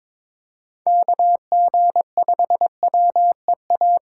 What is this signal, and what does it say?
Morse code, heard as a keyed tone, about 22 wpm, 700 Hz. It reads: KG5WEA
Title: USA Call Signs - 6 Characters - 3x Character Spacing - 22wpm